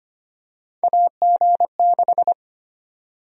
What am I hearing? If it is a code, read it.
Morse code, heard as a keyed tone, about 25 wpm, 700 Hz. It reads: AG6
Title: USA Call Sign Prefixes - 25wpm